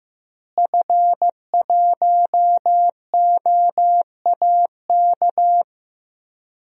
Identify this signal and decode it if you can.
Morse code, heard as a keyed tone, about 16 wpm, 700 Hz. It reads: F1OAK
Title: DXCC-100 Call Signs - 5 Characters - Repeated 3x - 15wpm